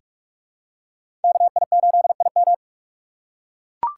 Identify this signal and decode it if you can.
Morse code, heard as a keyed tone, about 42 wpm, 700 Hz. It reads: KI8IM
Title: USA Call Signs - 3x Character Spacing - 45wpm